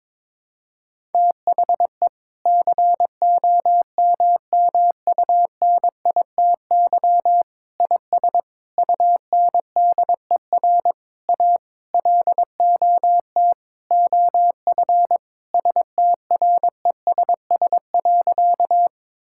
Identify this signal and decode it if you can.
Morse code, heard as a keyed tone, about 22 wpm, 700 Hz. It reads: THE COMMUNITY IS UNDER A LOT OF STRESS.
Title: Sentences from Top 1000 Words - Review Part 6 - 22wpm